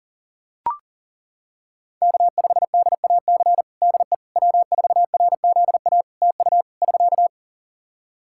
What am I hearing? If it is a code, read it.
Morse code, heard as a keyed tone, about 37 wpm, 700 Hz. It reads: K5DAC DE W4RZA TU <SK>
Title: Practice QSO - Exchange 4 - Ending - 40wpm